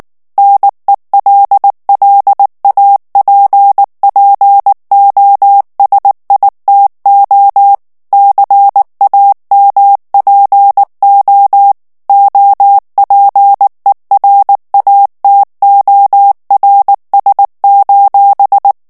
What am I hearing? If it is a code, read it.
Morse code, heard as a keyed tone, about 19 wpm, 800 Hz. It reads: NELLAPPOSITO CAMPO OPERATORS:
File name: morse mp3 / ITALIEN texte clair 2 vitesse 18 mots minute